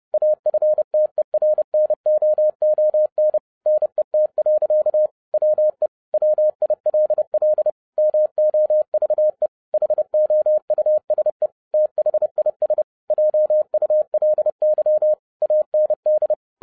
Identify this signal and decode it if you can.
Morse code, heard as a keyed tone, about 30 wpm, 610 Hz. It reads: AFTERNOON NET. WE WILL MOVE HOUSE THIS JULY AND